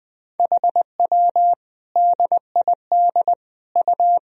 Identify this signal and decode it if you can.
Morse code, heard as a keyed tone, about 20 wpm, 700 Hz. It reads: HW DID U
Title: Sentences from Top 100 Words and Abbreviations 20wpm